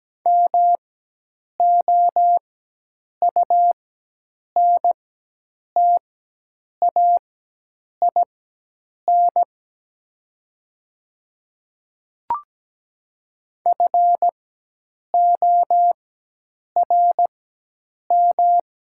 Morse code, heard as a keyed tone, about 17 wpm, spaced as for 8 wpm, 700 Hz. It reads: MOUNTAIN FORM
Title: 4x Character Spacing - Top 300 Words - 17wpm